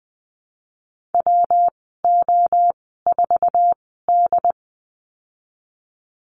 Morse code, heard as a keyed tone, about 19 wpm, spaced as for 14 wpm, 700 Hz. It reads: WO4D
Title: 207 Lesson 33g - Letter j - Call Signs 20wpm